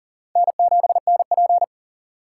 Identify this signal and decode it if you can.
Morse code, heard as a keyed tone, about 40 wpm, 700 Hz. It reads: N7NP
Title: USA Call Signs - 4 Characters - 4x Character Spacing - 40wpm